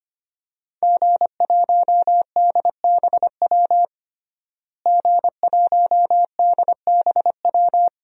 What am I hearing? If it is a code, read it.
Morse code, heard as a keyed tone, about 25 wpm, 700 Hz. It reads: G1DBW G1DBW
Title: DXCC-100 Call Signs - 5 Characters - Repeated 3x - 25wpm